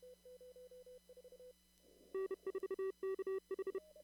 Morse code, heard as a keyed tone, about 31 wpm, 380 Hz. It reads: N4KH
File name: CW / kt5j wpx cw-op-k5pi